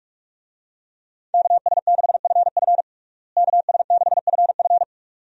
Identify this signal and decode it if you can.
Morse code, heard as a keyed tone, about 46 wpm, 700 Hz. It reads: KS6FF KS6FF
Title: USA Call Signs - 5 Characters - Repeated 2x - 45wpm